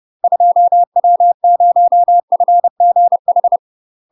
Morse code, heard as a keyed tone, about 30 wpm, 700 Hz. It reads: 2W0FGH
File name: UK MASTER QSOs TRAINING / 30WPM / UK MASTER QSOs 0045